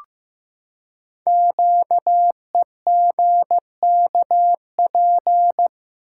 Morse code, heard as a keyed tone, about 15 wpm, 700 Hz. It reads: QEGKP
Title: Code Groups - 5 Letters - 15wpm